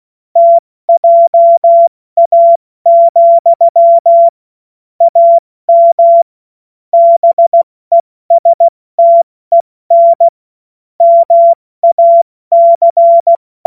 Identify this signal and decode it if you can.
Morse code, heard as a keyed tone, about 16 wpm, 680 Hz. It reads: TJA, AM BESTEN MAC
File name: Morsekurs / Witze / Witz 14. Dezember 2021, 70 F80